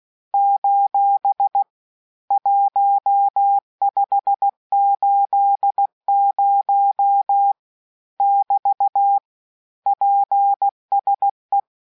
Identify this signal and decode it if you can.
Morse code, heard as a keyed tone, about 16 wpm, 800 Hz. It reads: : 1580 <BT> PSE
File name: cours-du-lundi files / cours-1580-abreviations-mpm-15